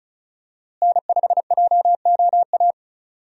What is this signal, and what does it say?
Morse code, heard as a keyed tone, about 34 wpm, 700 Hz. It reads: N5JOA
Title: USA Call Signs - 5 Characters - Repeated 3x - 35wpm